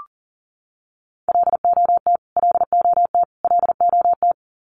Morse code, heard as a keyed tone, about 40 wpm, 700 Hz. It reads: LOT LOT LOT
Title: Top 500 Words Repeated 40wpm